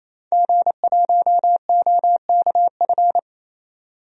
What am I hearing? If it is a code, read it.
Morse code, heard as a keyed tone, about 28 wpm, 700 Hz. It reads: G1OKF
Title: DXCC-100 Call Signs - 5 Characters - 28wpm